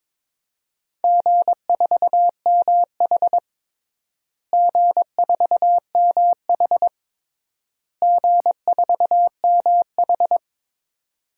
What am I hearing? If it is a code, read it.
Morse code, heard as a keyed tone, about 22 wpm, 700 Hz. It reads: G4MH G4MH G4MH
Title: DXCC-100 Call Signs - 4 Characters - Repeated 3x - 22wpm